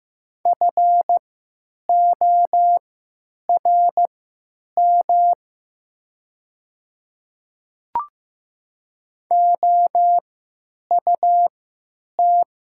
Morse code, heard as a keyed tone, about 15 wpm, spaced as for 9 wpm, 700 Hz. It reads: FORM OUT
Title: 3x Character Spacing - Top 300 Words - 15wpm